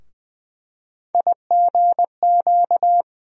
Morse code, heard as a keyed tone, about 20 wpm, 700 Hz. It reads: IGQ
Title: USA Call Sign Postfixes - 3 Letters 20wpm